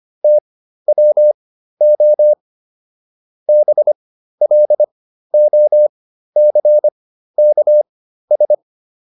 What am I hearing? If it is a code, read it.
Morse code, heard as a keyed tone, about 25 wpm, spaced as for 13 wpm, 600 Hz. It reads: TWO BLOCKS